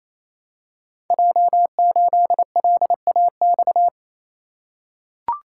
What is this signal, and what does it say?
Morse code, heard as a keyed tone, about 28 wpm, 700 Hz. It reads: J8LAX
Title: Code Groups - 5 Letter:Numbers - 28wpm